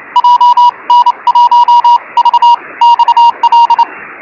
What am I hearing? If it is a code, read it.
Morse code, heard as a keyed tone, about 28 wpm, spaced as for 23 wpm, 950 Hz. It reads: JN1VXL
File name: FO29 JG8NQJ